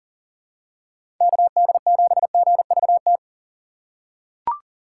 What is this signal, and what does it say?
Morse code, heard as a keyed tone, about 40 wpm, 700 Hz. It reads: KD7GVT
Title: USA Call Signs - 6 Characters - 2x Character Spacing - 40wpm